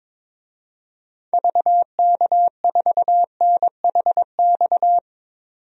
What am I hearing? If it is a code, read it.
Morse code, heard as a keyed tone, about 22 wpm, 700 Hz. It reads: VK4NHX
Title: DXCC-100 Call Signs - 6 Characters - 22wpm